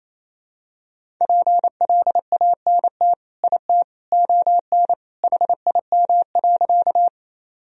Morse code, heard as a keyed tone, about 28 wpm, 700 Hz. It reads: PLANT IT ON HIM.
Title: Sentences from Top 1000 Words - Review Part 1 - 28wpm